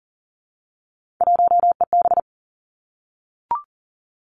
Morse code, heard as a keyed tone, about 42 wpm, 700 Hz. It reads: 1EB
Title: Three Challenging Letter-Number Combinations 40wpm